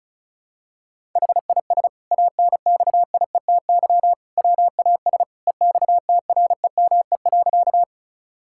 Morse code, heard as a keyed tone, about 36 wpm, 700 Hz. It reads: HIS ANXIETY WAS EXTREME.
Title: Sentences from Top 1600 Words 35wpm